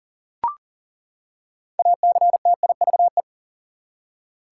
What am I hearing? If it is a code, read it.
Morse code, heard as a keyed tone, about 40 wpm, 700 Hz. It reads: ACTIVE
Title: Six Letter Words - 40wpm